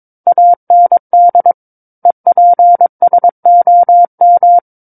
Morse code, heard as a keyed tone, about 22 wpm, 700 Hz. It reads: AND EPSOM
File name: Book One / 13-Chapter 13 - 22 WPM